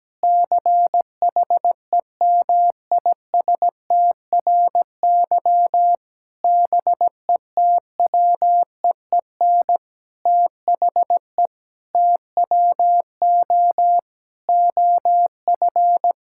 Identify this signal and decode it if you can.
Morse code, heard as a keyed tone, about 17 wpm, 700 Hz. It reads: CHEMISTRY BETWEEN THE TWO OF
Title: Sentences from Top 1000 Words - Review Part 6 - 17wpm